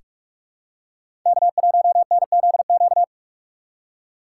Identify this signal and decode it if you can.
Morse code, heard as a keyed tone, about 41 wpm, 700 Hz. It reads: K1NZQ
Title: USA Call Signs - 5 Characters - 4x Character Spacing - 45wpm